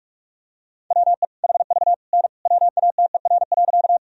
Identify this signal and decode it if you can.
Morse code, heard as a keyed tone, about 46 wpm, 700 Hz. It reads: WE HV N WATER.
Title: Sentences from Top 100 Words and Abbreviations 45wpm